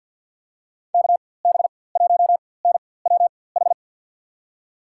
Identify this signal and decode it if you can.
Morse code, heard as a keyed tone, about 47 wpm, spaced as for 23 wpm, 700 Hz. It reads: KB1NWH
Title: USA Call Signs - 6 Characters - 4x Character Spacing - 50wpm